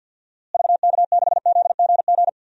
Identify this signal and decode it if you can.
Morse code, heard as a keyed tone, about 53 wpm, 700 Hz. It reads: VK6ZGG
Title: DXCC-100 Call Signs - 6 Characters - Repeated 2x - 50wpm